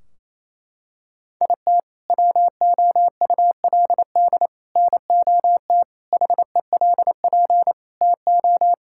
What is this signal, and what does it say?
Morse code, heard as a keyed tone, about 28 wpm, 700 Hz. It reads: IT WOULD NOT HELP TO
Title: Sentences from Top 1600 Words 28wpm